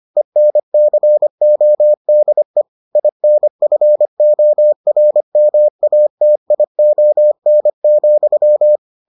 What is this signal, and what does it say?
Morse code, heard as a keyed tone, about 25 wpm, 600 Hz. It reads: ENCODE INFORMATION,